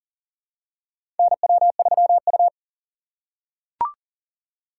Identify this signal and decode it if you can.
Morse code, heard as a keyed tone, about 40 wpm, 700 Hz. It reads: NW3U
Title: USA Call Signs - 4 Characters - 3x Character Spacing - 40wpm